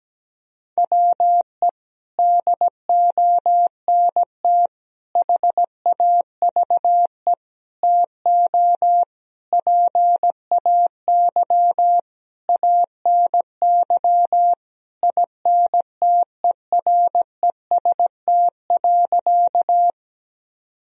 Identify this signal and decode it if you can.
Morse code, heard as a keyed tone, about 17 wpm, 700 Hz. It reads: WE DONT HAVE TO PAY ANY INTEREST.
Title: Sentences from Top 1000 Words - Review Part 2 - 17wpm